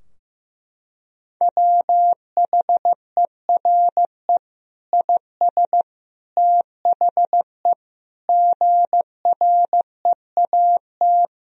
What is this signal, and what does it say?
Morse code, heard as a keyed tone, about 15 wpm, 700 Hz. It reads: WHERE IS THE GREAT
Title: Sentences from Top 1000 Words - Review Part 2 - 15wpm